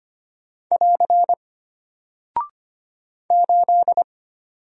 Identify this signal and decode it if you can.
Morse code, heard as a keyed tone, about 25 wpm, 700 Hz. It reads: <AR> 8
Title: Single Character (Letter:Number:Punctuation:Prosign) Alt 25wpm